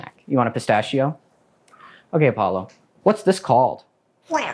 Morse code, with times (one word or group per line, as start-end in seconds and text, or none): none